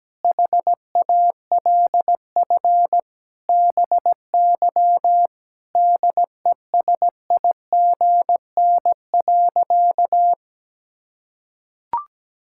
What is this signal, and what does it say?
Morse code, heard as a keyed tone, about 17 wpm, 700 Hz. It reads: HALF BY DESIGN.
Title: Sentences from Top 1000 Words - Review Part 5 - 17wpm